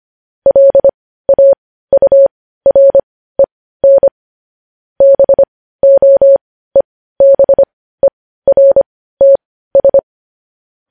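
Morse code, heard as a keyed tone, about 25 wpm, spaced as for 15 wpm, 550 Hz. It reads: LAUREN BOEBERTS
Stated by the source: Morse news headlines practice at 15 wpm.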